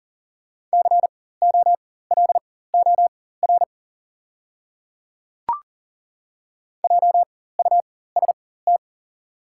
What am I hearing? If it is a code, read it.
Morse code, heard as a keyed tone, about 40 wpm, spaced as for 19 wpm, 700 Hz. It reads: COLOR JUST
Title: 4x Character Spacing - Top 300 Words - 40wpm